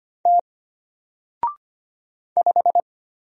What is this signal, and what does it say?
Morse code, heard as a keyed tone, about 25 wpm, 700 Hz. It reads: T 5
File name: Single-Letter-Number-25wpm